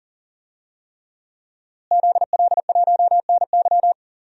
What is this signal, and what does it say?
Morse code, heard as a keyed tone, about 40 wpm, 700 Hz. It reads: ZL1NY
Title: DXCC-100 Call Signs - 5 Characters - Repeated 3x - 40wpm